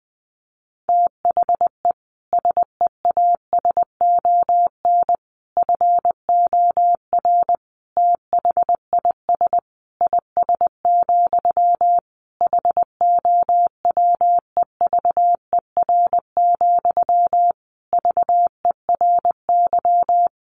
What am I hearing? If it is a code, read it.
Morse code, heard as a keyed tone, about 20 wpm, 700 Hz. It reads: THE SEASON FOR THIS IS, HOWEVER, VERY
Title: Sentences from Top 1400 Words 20wpm